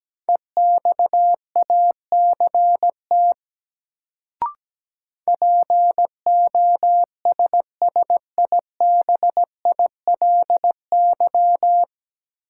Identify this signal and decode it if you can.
Morse code, heard as a keyed tone, about 17 wpm, 700 Hz. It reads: EXACT POSSIBILY
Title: Sentences from Top 1300 Words - New Word Drill Alt - 17wpm